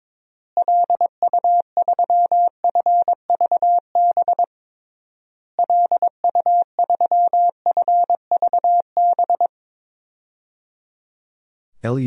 Morse code, heard as a keyed tone, about 22 wpm, 700 Hz. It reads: LU3FVB LU3FVB
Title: DXCC-100 Call Signs - 6 Characters - Repeated 3x - 22wpm